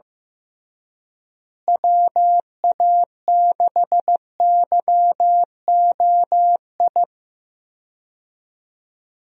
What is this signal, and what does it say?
Morse code, heard as a keyed tone, about 15 wpm, 700 Hz. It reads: WA6YOI
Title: USA Call Signs - 6 Characters - Repeated 3x - 15wpm